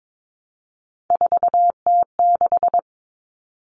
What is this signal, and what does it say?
Morse code, heard as a keyed tone, about 22 wpm, 700 Hz. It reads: 4T6
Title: Common Cut Numbers - 3-Digit 22wpm